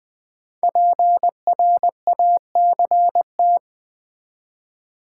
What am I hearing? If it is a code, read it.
Morse code, heard as a keyed tone, about 20 wpm, 700 Hz. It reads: PRACT
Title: Build A Word - 5 Letters At A Time - 12-Letter Words - 20wpm